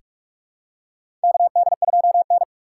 Morse code, heard as a keyed tone, about 46 wpm, 700 Hz. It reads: KD2N
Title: USA Call Signs - 4 Characters - 8x Character Spacing - 45wpm